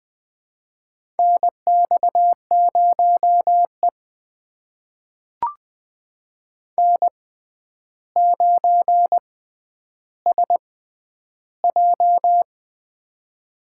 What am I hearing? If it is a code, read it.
Morse code, heard as a keyed tone, about 21 wpm, spaced as for 10 wpm, 700 Hz. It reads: NX0E N9SJ
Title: USA Call Signs - 6x Character Spacing - 20wpm